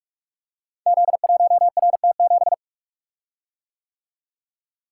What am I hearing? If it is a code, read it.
Morse code, heard as a keyed tone, about 45 wpm, 700 Hz. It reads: Z1RT7
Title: Code Groups - 5 Letter:Numbers - 45wpm